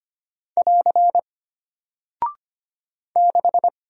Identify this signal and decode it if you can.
Morse code, heard as a keyed tone, about 25 wpm, 700 Hz. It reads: <AR> 6
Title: Single Character (Letter:Number:Punctuation:Prosign) Alt 25wpm